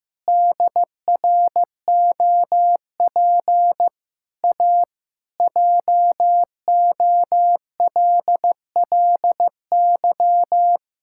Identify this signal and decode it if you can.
Morse code, heard as a keyed tone, about 16 wpm, 700 Hz. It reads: DROP A JOLLY
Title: Pangram Challenge v2 - 15wpm